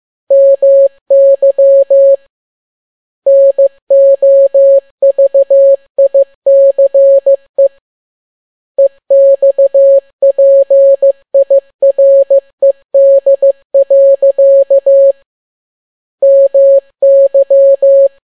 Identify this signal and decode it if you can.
Morse code, heard as a keyed tone, about 15 wpm, 550 Hz. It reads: MY NOVICE EXPIRED. MY
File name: n6tt p1 15C13